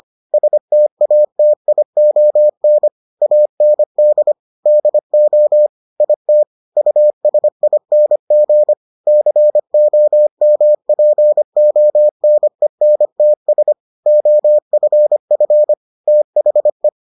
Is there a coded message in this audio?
Yes: Morse code from 0.3 to 16.9 s, heard as a keyed tone, about 25 wpm, 600 Hz. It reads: STATION AND DO IT USING COMPONENTS OFF THE